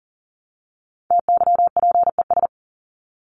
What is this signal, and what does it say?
Morse code, heard as a keyed tone, about 40 wpm, 700 Hz. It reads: TYPES
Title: ES:S Suffix - Top 100 Words with Base Words 40wpm